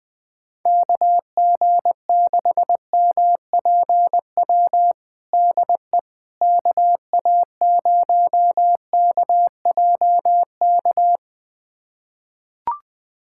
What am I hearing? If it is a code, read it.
Morse code, heard as a keyed tone, about 20 wpm, 700 Hz. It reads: KG6MPW DE KA0KJK
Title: USA Callsign DE Callsign - 6 Characters - 20wpm